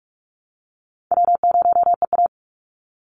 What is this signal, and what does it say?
Morse code, heard as a keyed tone, about 44 wpm, 700 Hz. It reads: W0EA
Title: Call Signs Repeated 2x - 45wpm